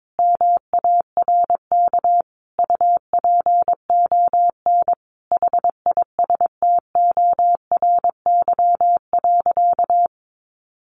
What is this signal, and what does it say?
Morse code, heard as a keyed tone, about 22 wpm, 700 Hz. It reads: MARK UPON HISTORY.
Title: Sentences from Top 1100 Words 22wpm